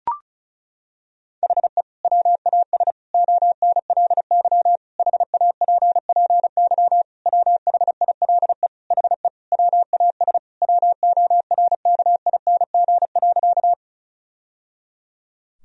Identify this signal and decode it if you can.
Morse code, heard as a keyed tone, about 35 wpm, 700 Hz. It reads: HE WAS ONLY HAPPY WHILE HE WAS WORKING.
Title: Sentences from Top 1000 Words - Review Part 4 - 35wpm